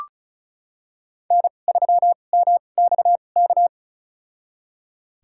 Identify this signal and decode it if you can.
Morse code, heard as a keyed tone, about 34 wpm, spaced as for 25 wpm, 700 Hz. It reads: N3MXK
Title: USA Call Signs - 2x Character Spacing - 35wpm